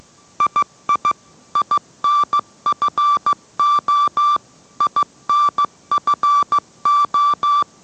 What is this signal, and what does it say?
Morse code, heard as a keyed tone, about 15 wpm, 1170 Hz. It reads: II INFO INFO